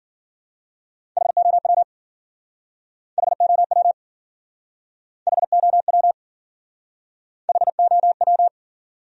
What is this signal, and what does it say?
Morse code, heard as a keyed tone, about 53 wpm, 700 Hz. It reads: HOW HOW HOW HOW